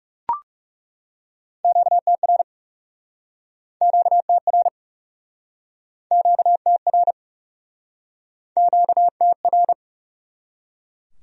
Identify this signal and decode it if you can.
Morse code, heard as a keyed tone, about 40 wpm, 700 Hz. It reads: QTR QTR QTR QTR